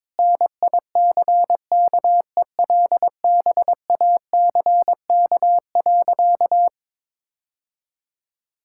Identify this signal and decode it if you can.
Morse code, heard as a keyed tone, about 22 wpm, 700 Hz. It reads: NICKELBACK.
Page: morse-code-ninja-practice-22wpm